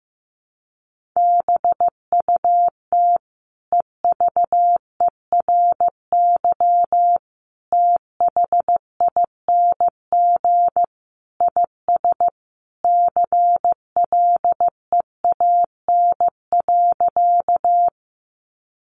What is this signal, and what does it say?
Morse code, heard as a keyed tone, about 15 wpm, 700 Hz. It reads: BUT EVERYTHING IS CLEAN.
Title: Sentences from Top 1100 Words 15wpm